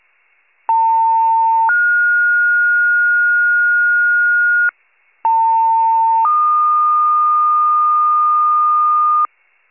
Two-tone page, heard as a one-second tone, then a three-second tone, about 900 then 1430 Hz, about 900 then 1220 Hz, with noise. It is heard from 0.7 to 9.3 s.